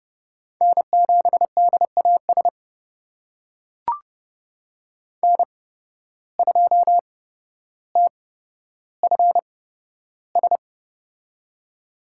Morse code, heard as a keyed tone, about 31 wpm, spaced as for 13 wpm, 700 Hz. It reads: N7DAS N2TFS
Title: USA Call Signs - 8x Character Spacing - 30wpm